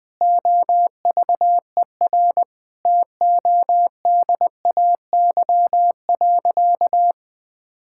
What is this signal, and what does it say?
Morse code, heard as a keyed tone, about 20 wpm, 700 Hz. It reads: OVER TODAY.
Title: Sentences with 4-Word Phrases - 20wpm